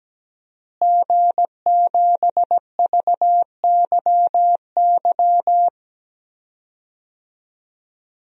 Morse code, heard as a keyed tone, about 17 wpm, 700 Hz. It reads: G7VYY
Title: DXCC-100 Call Signs - 5 Characters - Repeated 2x - 17wpm